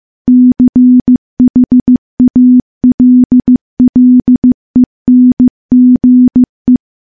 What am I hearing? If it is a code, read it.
Morse code, heard as a keyed tone, about 15 wpm, 250 Hz. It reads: CHALLENGE
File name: morse-training-15WPM